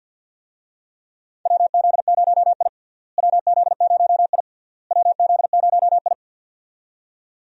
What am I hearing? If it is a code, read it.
Morse code, heard as a keyed tone, about 47 wpm, 700 Hz. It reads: WZ0I WZ0I WZ0I
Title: USA Call Signs - 4 Characters - Repeated 3x - 50wpm